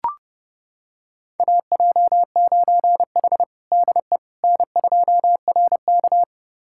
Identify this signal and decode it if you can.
Morse code, heard as a keyed tone, about 30 wpm, 700 Hz. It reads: AJ9H DE N2RK
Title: USA Callsign DE Callsign - 4 Characters - 30wpm